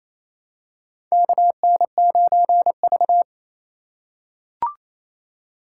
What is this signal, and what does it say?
Morse code, heard as a keyed tone, about 27 wpm, 700 Hz. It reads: KN9V
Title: USA Call Signs - 4 Characters - 3x Character Spacing - 28wpm